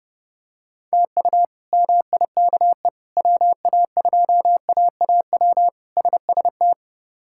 Morse code, heard as a keyed tone, about 30 wpm, 700 Hz. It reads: TU MIKE WA2AAW SST
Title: K1USN SST QSO Practice - 30wpm